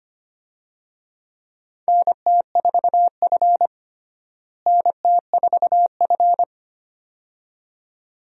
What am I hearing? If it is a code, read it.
Morse code, heard as a keyed tone, about 25 wpm, 700 Hz. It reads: NT4F NT4F
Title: USA Call Signs - 4 Characters - Repeated 2x - 25wpm